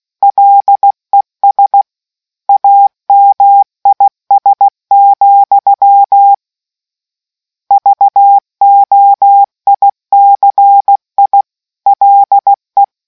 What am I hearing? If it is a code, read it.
Morse code, heard as a keyed tone, about 16 wpm, 800 Hz. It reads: LES AMIS, VOICI LE